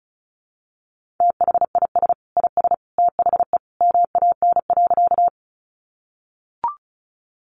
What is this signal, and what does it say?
Morse code, heard as a keyed tone, about 34 wpm, 700 Hz. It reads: THIS IS THE MAN.
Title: Sentences from Top 200 Words 35wpm